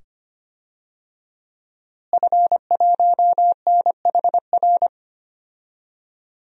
Morse code, heard as a keyed tone, about 25 wpm, 700 Hz. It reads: F1NHR
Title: DXCC-100 Call Signs - 5 Characters - Repeated 2x - 25wpm